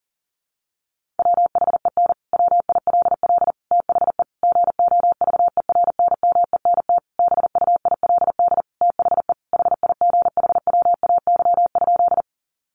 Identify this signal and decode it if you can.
Morse code, heard as a keyed tone, about 40 wpm, 700 Hz. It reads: WHEN WILL THE GOVERNMENT BUILD THE HIGHWAY?
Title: Sentences from Top 1400 Words 40wpm